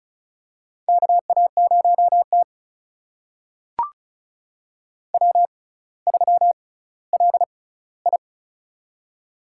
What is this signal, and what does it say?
Morse code, heard as a keyed tone, about 36 wpm, spaced as for 18 wpm, 700 Hz. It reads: KA0T W3LI
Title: USA Call Signs - 4 Characters - 6x Character Spacing - 35wpm